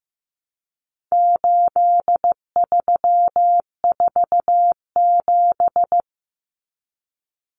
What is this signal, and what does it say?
Morse code, heard as a keyed tone, about 15 wpm, 700 Hz. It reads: 8347
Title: Numbers - 4-Digit 15wpm